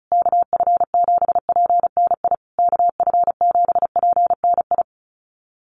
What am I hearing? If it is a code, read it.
Morse code, heard as a keyed tone, about 34 wpm, 700 Hz. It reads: KF7PNI KF7PNI
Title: USA Call Signs Repeated 3x 35wpm